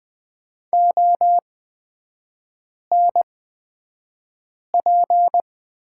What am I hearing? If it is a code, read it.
Morse code, heard as a keyed tone, about 20 wpm, spaced as for 5 wpm, 700 Hz. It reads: ONP
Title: 224 Lesson 36e - BT - Three Characters 20wpm